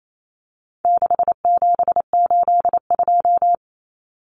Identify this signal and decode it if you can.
Morse code, heard as a keyed tone, about 28 wpm, 700 Hz. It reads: 6782
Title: Common Cut Numbers - 4-Digit 28wpm